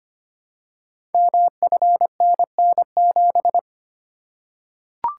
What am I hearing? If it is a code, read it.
Morse code, heard as a keyed tone, about 25 wpm, 700 Hz. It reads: MFNN7
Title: Code Groups - 5 Letter:Numbers - 25wpm